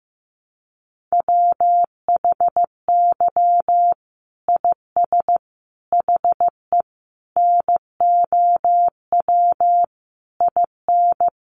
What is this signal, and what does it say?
Morse code, heard as a keyed tone, about 15 wpm, 700 Hz. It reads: WHY IS HE NOW IN
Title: Sentences from Top 1300 Words 15wpm